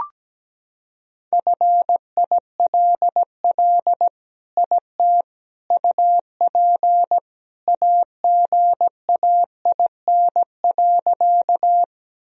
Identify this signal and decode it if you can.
Morse code, heard as a keyed tone, about 17 wpm, 700 Hz. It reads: FILL IT UP AGAIN.
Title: Sentences from Top 1000 Words - Review Part 3 - 17wpm